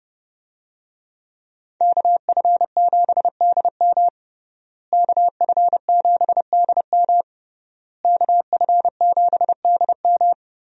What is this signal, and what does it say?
Morse code, heard as a keyed tone, about 30 wpm, 700 Hz. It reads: KF7DM KF7DM KF7DM
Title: USA Call Signs - 5 Characters - Repeated 3x - 30wpm